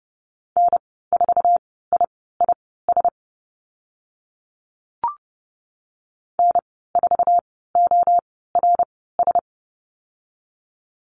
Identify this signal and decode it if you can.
Morse code, heard as a keyed tone, about 30 wpm, spaced as for 17 wpm, 700 Hz. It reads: N4IIS N4ORS
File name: Lesson 14g - Number 2 - Call Signs 30wpm